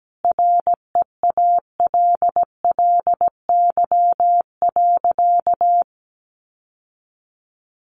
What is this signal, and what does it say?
Morse code, heard as a keyed tone, about 17 wpm, 700 Hz. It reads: REALLY.
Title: Sentences from Top 600 Words 17wpm